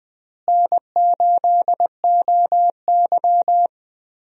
Morse code, heard as a keyed tone, about 20 wpm, 700 Hz. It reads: N8OY
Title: USA Call Signs - 4 Characters - Repeated 2x - 20wpm